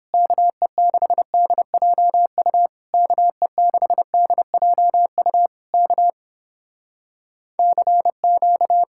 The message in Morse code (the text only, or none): KE6DJU KE6DJU K CQ